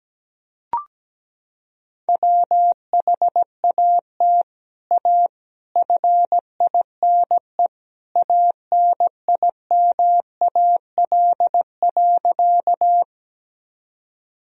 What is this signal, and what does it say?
Morse code, heard as a keyed tone, about 17 wpm, 700 Hz. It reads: WHAT A FINE ANIMAL.
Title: Sentences from Top 1000 Words - Review Part 1 - 17wpm